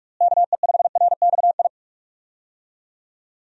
Morse code, heard as a keyed tone, about 44 wpm, 700 Hz. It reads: KE5RXI
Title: USA Call Signs - 6 Characters - Repeated 3x - 45wpm